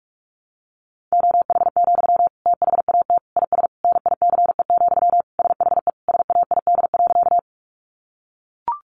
Morse code, heard as a keyed tone, about 44 wpm, 700 Hz. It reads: OH, THAT IS NICE, SHE SAID.
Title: Sentences from Top 1100 Words 45wpm